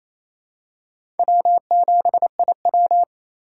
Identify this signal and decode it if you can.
Morse code, heard as a keyed tone, about 28 wpm, 700 Hz. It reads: W7IW
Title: USA Call Signs - 4 Characters - 3x Character Spacing - 28wpm